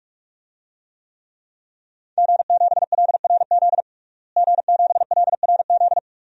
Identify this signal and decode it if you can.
Morse code, heard as a keyed tone, about 46 wpm, 700 Hz. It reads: G7LRZ G7LRZ
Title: DXCC-100 Call Signs - 5 Characters - Repeated 2x - 45wpm